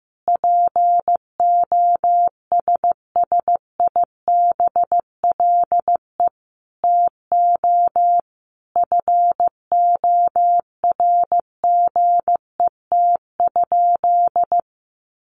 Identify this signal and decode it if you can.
Morse code, heard as a keyed tone, about 15 wpm, 700 Hz. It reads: POSSIBLE TO FORGET?
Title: Sentences from Top 1200 Words 15wpm